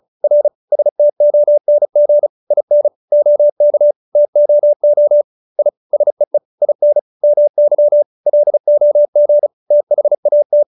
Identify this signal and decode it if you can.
Morse code, heard as a keyed tone, about 34 wpm, 590 Hz. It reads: R STONG IN OK TOO I SEE IN MY LOG THAT